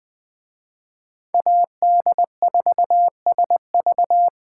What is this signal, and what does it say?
Morse code, heard as a keyed tone, about 24 wpm, 700 Hz. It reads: AD4SV
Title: Speed Racing - USA Call Signs 20wpm